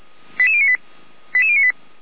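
Five-tone selective call (ZVEI), heard as quick runs of five tones, tones of about 70 ms. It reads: 88098 88098
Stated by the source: ZVEI five-tone selective call 88083 received via SDR.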